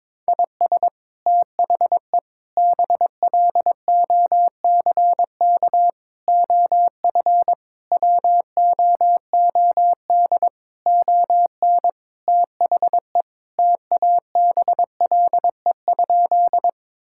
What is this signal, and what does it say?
Morse code, heard as a keyed tone, about 22 wpm, 700 Hz. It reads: IS THE BLOCK OF WOOD ON THE TABLE?
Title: Sentences from Top 1600 Words 22wpm